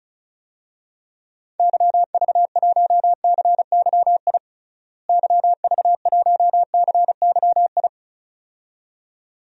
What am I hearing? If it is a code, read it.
Morse code, heard as a keyed tone, about 34 wpm, 700 Hz. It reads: YV1CYI YV1CYI
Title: DXCC-100 Call Signs - 6 Characters - Repeated 2x - 35wpm